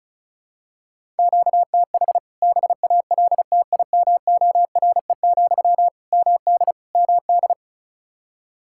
Morse code, heard as a keyed tone, about 34 wpm, 700 Hz. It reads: QTH BALTIMORE, MD MD
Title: K1USN SST QSO Practice 35wpm